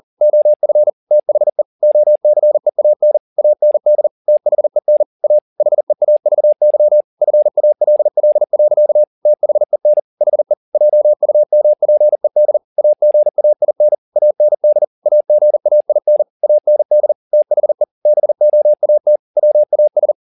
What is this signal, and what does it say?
Morse code, heard as a keyed tone, about 40 wpm, 600 Hz. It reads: OF THE OCEAN AND THEN A HEAVY FALL. THEN HE JUMPED AGAIN AND AGAIN AND THE BOAT WAS
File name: the old man and the sea 2 40wpm